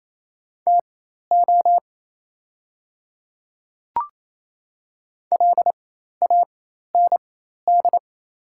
Morse code, heard as a keyed tone, about 27 wpm, spaced as for 13 wpm, 690 Hz. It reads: TO LAND
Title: 4x Character Spacing - Top 200 Words - 28wpm